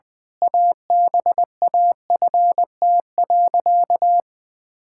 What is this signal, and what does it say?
Morse code, heard as a keyed tone, about 21 wpm, 700 Hz. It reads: ABAFT.
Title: Pangram Challenge v2 - 20wpm